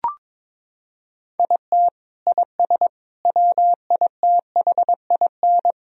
Morse code, heard as a keyed tone, about 22 wpm, 700 Hz. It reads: IT IS WITHIN
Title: Sentences from Top 1000 Words - Review Part 4 - 22wpm